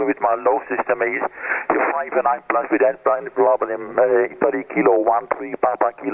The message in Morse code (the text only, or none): none